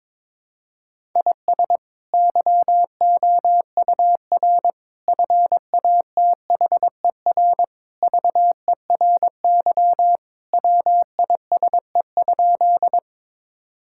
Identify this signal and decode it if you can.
Morse code, heard as a keyed tone, about 22 wpm, 700 Hz. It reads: IS YOUR FATHER VERY WISE?
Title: Sentences from Top 1600 Words 22wpm